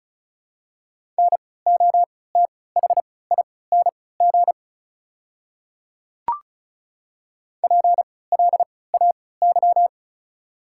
Morse code, heard as a keyed tone, about 36 wpm, spaced as for 20 wpm, 690 Hz. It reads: NOTHING PLAY
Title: 3x Character Spacing - Top 500 Words - 35wpm